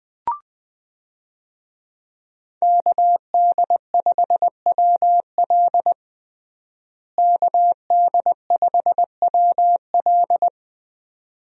Morse code, heard as a keyed tone, about 20 wpm, 700 Hz. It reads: KD5WL KD5WL
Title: USA Call Signs - 5 Characters - Repeated 2x - 20wpm